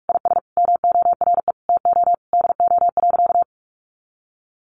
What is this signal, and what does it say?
Morse code, heard as a keyed tone, about 44 wpm, 700 Hz. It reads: IS MORE TO DO.
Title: Sentences from Top 100 Words 45wpm